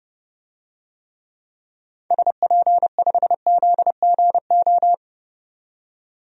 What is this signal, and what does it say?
Morse code, heard as a keyed tone, about 30 wpm, 700 Hz. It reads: SP5ZGO
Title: DXCC-100 Call Signs - 6 Characters - 30wpm